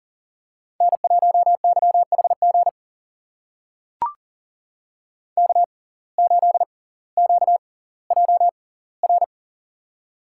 Morse code, heard as a keyed tone, about 39 wpm, spaced as for 20 wpm, 700 Hz. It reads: N1YHG K8QJR
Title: USA Call Signs - 5 Characters - 6x Character Spacing - 40wpm